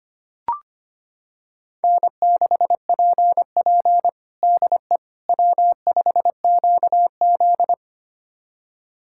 Morse code, heard as a keyed tone, about 25 wpm, 700 Hz. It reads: N6PP DE W5QZ
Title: USA Callsign DE Callsign - 4 Characters - 25wpm